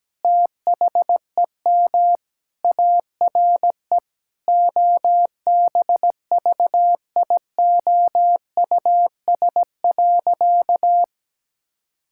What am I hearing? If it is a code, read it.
Morse code, heard as a keyed tone, about 17 wpm, 700 Hz. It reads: THEM ARE OBVIOUS.
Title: Sentences from Top 1600 Words 17wpm